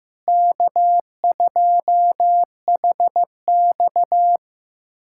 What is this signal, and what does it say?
Morse code, heard as a keyed tone, about 16 wpm, 700 Hz. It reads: K2HX
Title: USA Callsign DE Callsign - 4 Characters - 15wpm